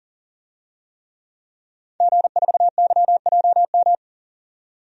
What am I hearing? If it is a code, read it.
Morse code, heard as a keyed tone, about 40 wpm, 700 Hz. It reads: G4YJM
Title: DXCC-100 Call Signs - 5 Characters - 40wpm